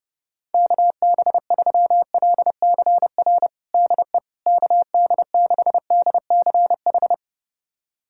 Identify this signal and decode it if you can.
Morse code, heard as a keyed tone, about 30 wpm, 700 Hz. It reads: KB3LCR DE KD6DCH
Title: USA Callsign DE Callsign - 6 Characters - 30wpm